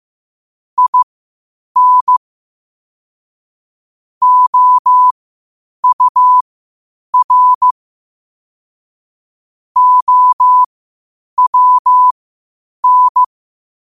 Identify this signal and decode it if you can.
Morse code, heard as a keyed tone, about 15 wpm, spaced as for 8 wpm, 1000 Hz. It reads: IN OUR OWN
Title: 8 WPM morse code quotes for Fri, 01 Mar 2024